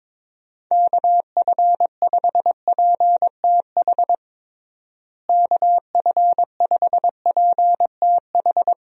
Morse code, heard as a keyed tone, about 22 wpm, 700 Hz. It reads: KF5PTH KF5PTH
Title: USA Call Signs - 6 Characters - Repeated 2x - 22wpm